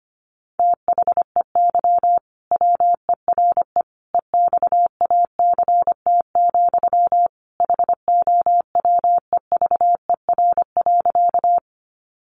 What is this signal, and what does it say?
Morse code, heard as a keyed tone, about 25 wpm, 700 Hz. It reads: THEY WERE EXACT, HOWEVER.
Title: Sentences from Top 1300 Words 25wpm